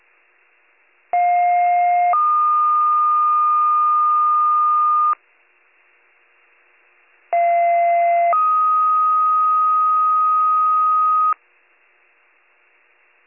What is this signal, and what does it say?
Two-tone page, heard as a one-second tone, then a three-second tone, about 710 then 1220 Hz, about 690 then 1250 Hz. It is heard from 1.1 to 11.3 s.